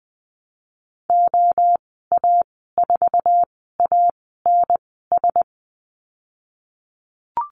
Morse code, heard as a keyed tone, about 19 wpm, spaced as for 14 wpm, 700 Hz. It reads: OA4ANS
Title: 048 Lesson 9g - Number 4 - Call Signs 20wpm